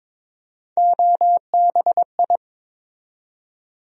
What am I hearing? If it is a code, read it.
Morse code, heard as a keyed tone, about 22 wpm, 700 Hz. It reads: OBI
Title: USA Call Sign Postfixes - 3 Letters 22wpm